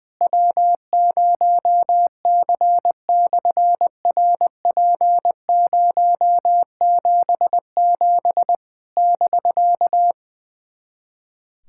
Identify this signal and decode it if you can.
Morse code, heard as a keyed tone, about 20 wpm, 700 Hz. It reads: W0C/RP077 <BK>
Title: SOTA QSO Practice 20wpm